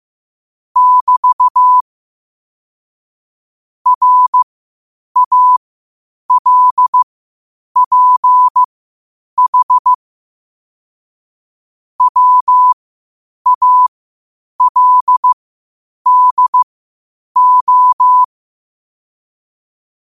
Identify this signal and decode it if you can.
Morse code, heard as a keyed tone, about 15 wpm, spaced as for 8 wpm, 1000 Hz. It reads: <BT> RALPH WALDO
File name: qotdfiles / 8wpm